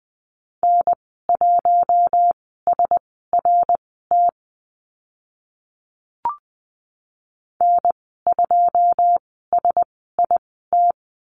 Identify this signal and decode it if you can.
Morse code, heard as a keyed tone, about 20 wpm, spaced as for 15 wpm, 700 Hz. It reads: N1SRT N2SIT
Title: 082 Lesson 14g - Number 2 - Call Signs 20wpm